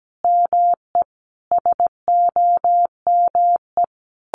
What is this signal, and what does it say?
Morse code, heard as a keyed tone, about 17 wpm, 700 Hz. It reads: ME SOME
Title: Sentences from Top 1500 Words 17wpm